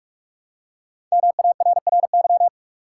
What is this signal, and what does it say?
Morse code, heard as a keyed tone, about 46 wpm, 700 Hz. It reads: MARRY
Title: Five Letter Words - 45wpm